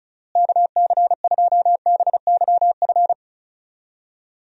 Morse code, heard as a keyed tone, about 36 wpm, 700 Hz. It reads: KC2BYF
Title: USA Call Signs - 6 Characters - Repeated 3x - 35wpm